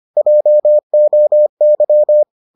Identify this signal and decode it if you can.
Morse code, heard as a keyed tone, about 25 wpm, 600 Hz. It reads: JOY